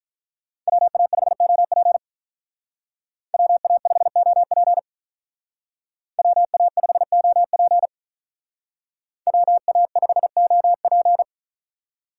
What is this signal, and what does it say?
Morse code, heard as a keyed tone, about 45 wpm, 700 Hz. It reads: WA5OP WA5OP WA5OP WA5OP